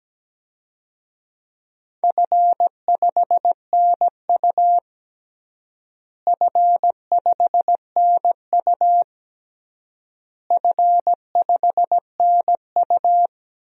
Morse code, heard as a keyed tone, about 17 wpm, 700 Hz. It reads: F5NU F5NU F5NU
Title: DXCC-100 Call Signs - 4 Characters - Repeated 3x - 17wpm